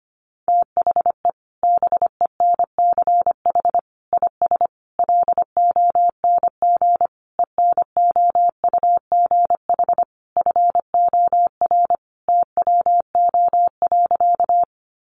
Morse code, heard as a keyed tone, about 25 wpm, 700 Hz. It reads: THE BENCH IS LONG ENOUGH FOR TWO.
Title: Sentences from Top 1400 Words 25wpm